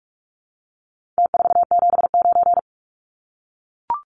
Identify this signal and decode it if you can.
Morse code, heard as a keyed tone, about 44 wpm, 700 Hz. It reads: T479
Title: Numbers - Contest Style - 4-Digit 45wpm